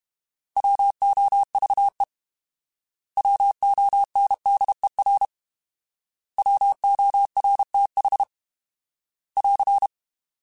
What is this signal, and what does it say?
Morse code, heard as a keyed tone, about 32 wpm, 800 Hz. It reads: WOVE WONDER WORTH <AR>
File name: Alphabetical / Alphabetical mp3s / Ws @30